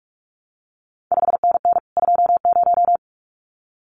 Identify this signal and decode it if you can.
Morse code, heard as a keyed tone, about 46 wpm, 700 Hz. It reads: 5NN 20
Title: CQ WW DX Contest Exchange - 45wpm